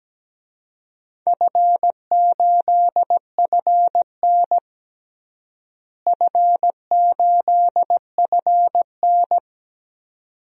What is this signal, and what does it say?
Morse code, heard as a keyed tone, about 17 wpm, 700 Hz. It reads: F8FN F8FN
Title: DXCC-100 Call Signs - 4 Characters - Repeated 2x - 17wpm